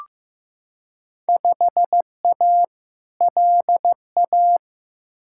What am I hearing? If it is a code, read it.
Morse code, heard as a keyed tone, about 16 wpm, 700 Hz. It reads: 5A LA
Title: Field Day Exchanges Encore - 15wpm